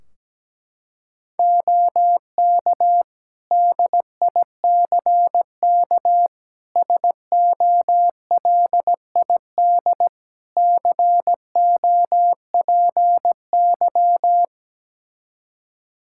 Morse code, heard as a keyed tone, about 17 wpm, 700 Hz. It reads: OK DICK SOLID COPY
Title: Practice QSO - Exchange 3 - More Info - 17wpm